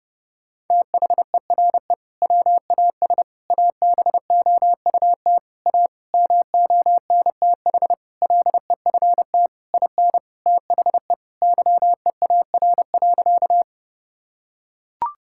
Morse code, heard as a keyed tone, about 31 wpm, 700 Hz. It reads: THERE WAS ABOUT A MONTH LEFT IN THE YEAR.
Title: Sentences from Top 1000 Words - Review Part 5 - 30wpm